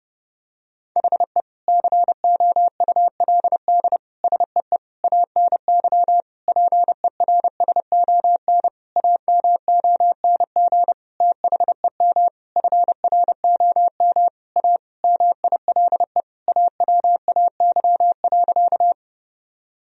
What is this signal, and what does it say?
Morse code, heard as a keyed tone, about 30 wpm, 700 Hz. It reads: HE COULD SEE ANY PERSON AMONG THEM FROM A MILE AWAY.
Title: Sentences from Top 1000 Words - Review Part 3 - 30wpm